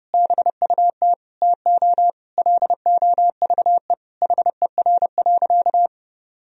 Morse code, heard as a keyed tone, about 30 wpm, 700 Hz. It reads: BUT TO LOVE HER.
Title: Sentences from Top 1000 Words - Review Part 4 - 30wpm